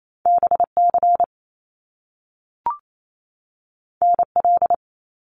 Morse code, heard as a keyed tone, about 28 wpm, 700 Hz. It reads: BC NL
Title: Canadian Provinces and Territories 28wpm